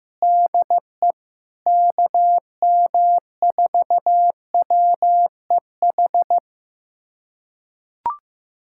Morse code, heard as a keyed tone, about 15 wpm, 700 Hz. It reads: DE KM4WEH
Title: USA Callsign DE Callsign 15wpm